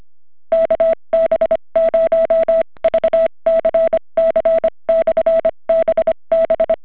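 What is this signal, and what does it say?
Morse code, heard as a keyed tone, about 25 wpm, spaced as for 21 wpm, 660 Hz. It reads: KB0VCC/BB
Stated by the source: Portable CW from Pawnee National Grasslands, Colorado, calling "CQ BB de KB0VCC/BB" near 7.040 in Adventure Radio Society Bumblebee event; OHR-100 backpack rig, inverted VEE, rain and wind.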